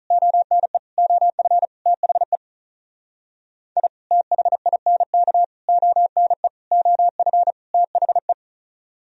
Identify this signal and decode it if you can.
Morse code, heard as a keyed tone, about 46 wpm, 700 Hz. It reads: ONE OF THE I THINK ONE OF THE